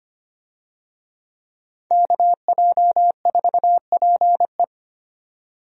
Morse code, heard as a keyed tone, about 25 wpm, 700 Hz. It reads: KJ4PE
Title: USA Call Signs - 5 Characters - 25wpm